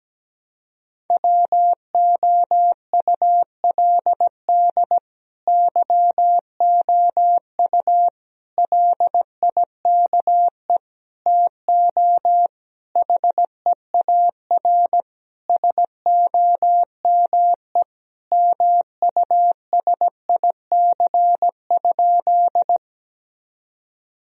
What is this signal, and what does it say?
Morse code, heard as a keyed tone, about 17 wpm, 700 Hz. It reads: WOULD YOU LIKE TO HEAR SOME MUSIC?
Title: Sentences from Top 1000 Words - Review Part 1 - 17wpm